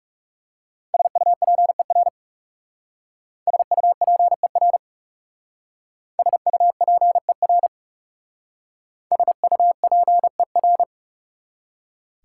Morse code, heard as a keyed tone, about 36 wpm, 690 Hz. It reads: SUPER SUPER SUPER SUPER
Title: Speed-Racing Prefixes and Suffixes 30wpm